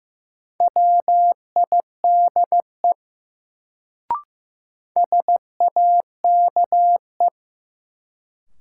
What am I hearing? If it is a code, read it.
Morse code, heard as a keyed tone, about 15 wpm, 700 Hz. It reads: WIDE SAKE
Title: Sets of 4 Words - New Word Drill - 15wpm